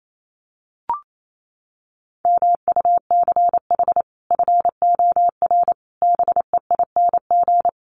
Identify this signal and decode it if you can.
Morse code, heard as a keyed tone, about 28 wpm, 700 Hz. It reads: MUCH FOR BEING
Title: Sets of 5 Words 28wpm